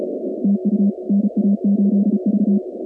Morse code, heard as a keyed tone, about 34 wpm, 200 Hz. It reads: TU NA8V
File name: contests / 2019-cw-wpx-kt5j-k5pi- -2330 0-46